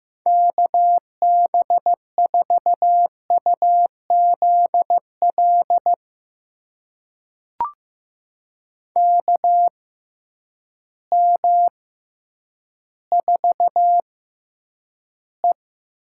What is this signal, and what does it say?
Morse code, heard as a keyed tone, about 15 wpm, spaced as for 9 wpm, 700 Hz. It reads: KB4UZL KM4E
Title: USA Call Signs - 6x Character Spacing - 15wpm